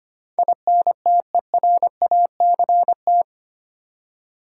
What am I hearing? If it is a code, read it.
Morse code, heard as a keyed tone, about 25 wpm, 700 Hz. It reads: INTERACT
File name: Eight Letter Words - 25wpm